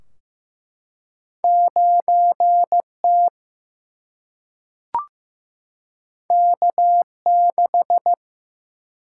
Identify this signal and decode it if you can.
Morse code, heard as a keyed tone, about 16 wpm, 700 Hz. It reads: 9 T K 6
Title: Single-Letter Single-Number 15wpm